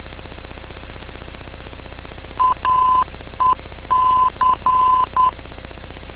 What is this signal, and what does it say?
Morse code, heard as a keyed tone, about 10 wpm, 1020 Hz. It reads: AEC
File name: info / aec vor